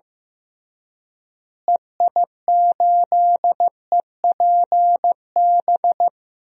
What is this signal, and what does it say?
Morse code, heard as a keyed tone, about 16 wpm, 700 Hz. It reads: EI8EPB
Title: DXCC-100 Call Signs - 6 Characters - Repeated 2x - 15wpm